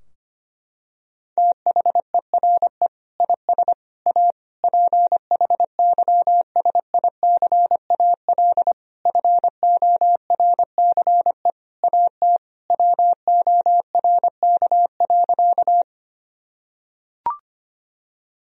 Morse code, heard as a keyed tone, about 25 wpm, 700 Hz. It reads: THERE IS A PHYSICAL FORCE AT WORK.
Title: Sentences from Top 1000 Words - Review Part 4 - 25wpm